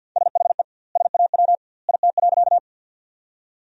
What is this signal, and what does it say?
Morse code, heard as a keyed tone, about 50 wpm, 700 Hz. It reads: SHE SAW IT.
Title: 2x Word-Spacing - Sentences from Top 300 Words - 50wpm